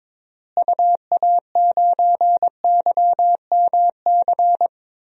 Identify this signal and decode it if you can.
Morse code, heard as a keyed tone, about 22 wpm, 700 Hz. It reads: UA9YMC
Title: DXCC-100 Call Signs - 6 Characters - Repeated 2x - 22wpm